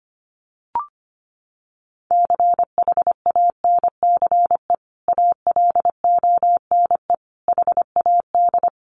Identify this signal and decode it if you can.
Morse code, heard as a keyed tone, about 25 wpm, 700 Hz. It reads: CHANCE ALONE HAD
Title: Sentences from Top 1000 Words 25wpm